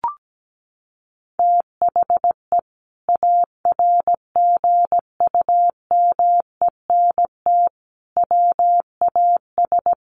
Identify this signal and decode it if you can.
Morse code, heard as a keyed tone, about 17 wpm, 700 Hz. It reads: THE ARGUMENT WAS
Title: Sentences from Top 1400 Words 17wpm